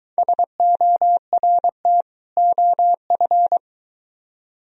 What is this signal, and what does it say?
Morse code, heard as a keyed tone, about 23 wpm, 700 Hz. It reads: SORT OF